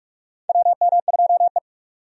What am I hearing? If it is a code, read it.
Morse code, heard as a keyed tone, about 46 wpm, 700 Hz. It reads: WM2E
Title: USA Call Signs - 4 Characters - 45wpm